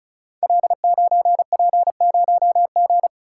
Morse code, heard as a keyed tone, about 34 wpm, 690 Hz. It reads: L9P0G
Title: Code Groups - 5 Letter:Numbers - 35wpm